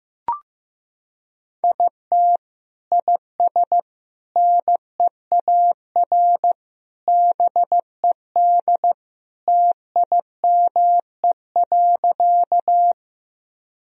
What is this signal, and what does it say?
Morse code, heard as a keyed tone, about 15 wpm, 700 Hz. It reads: IT IS NEAR BED TIME.
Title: Sentences from Top 1000 Words - Review Part 1 - 15wpm